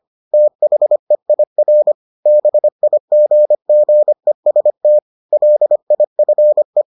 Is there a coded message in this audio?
Yes: Morse code from 0.3 to 6.8 s, heard as a keyed tone, about 25 wpm, 600 Hz. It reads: THEIR BIGGEST LIFE